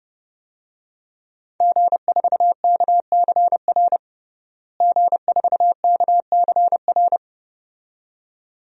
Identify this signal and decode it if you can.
Morse code, heard as a keyed tone, about 29 wpm, 700 Hz. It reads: G4KCR G4KCR
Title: DXCC-100 Call Signs - 5 Characters - Repeated 2x - 30wpm